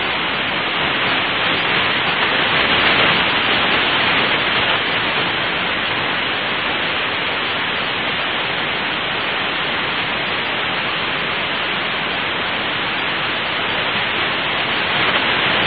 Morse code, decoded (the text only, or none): none